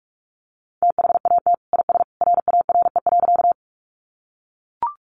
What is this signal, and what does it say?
Morse code, heard as a keyed tone, about 44 wpm, 700 Hz. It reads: THAT IS RARE.
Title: Sentences from Top 1100 Words 45wpm